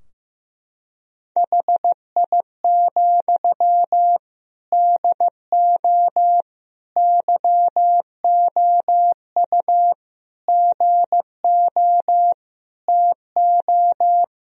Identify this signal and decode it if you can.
Morse code, heard as a keyed tone, about 15 wpm, 700 Hz. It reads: HI, DO YOU GO TO
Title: Sentences from Top 1000 Words - Review Part 3 - 15wpm